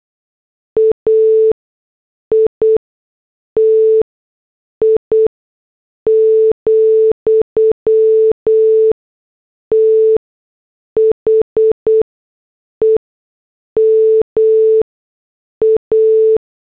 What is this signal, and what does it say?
Morse code, heard as a keyed tone, about 8 wpm, spaced as for 6 wpm, 440 Hz. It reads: AITI, THE MA
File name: morse / headlines